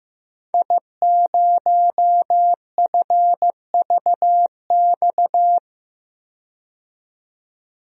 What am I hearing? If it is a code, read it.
Morse code, heard as a keyed tone, about 15 wpm, 700 Hz. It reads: I0FVX
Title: DXCC-100 Call Signs - 5 Characters - 15wpm